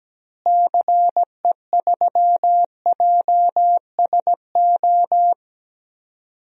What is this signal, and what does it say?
Morse code, heard as a keyed tone, about 17 wpm, 700 Hz. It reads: CE3JSO
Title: DXCC-100 Call Signs - 6 Characters - Repeated 2x - 17wpm